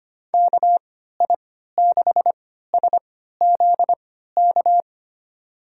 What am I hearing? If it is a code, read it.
Morse code, heard as a keyed tone, about 25 wpm, spaced as for 14 wpm, 700 Hz. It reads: KI6SZK
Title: USA Call Signs - 6 Characters - 3x Character Spacing - 25wpm